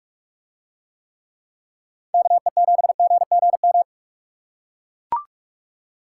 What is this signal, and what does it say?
Morse code, heard as a keyed tone, about 44 wpm, 700 Hz. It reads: KE7GGM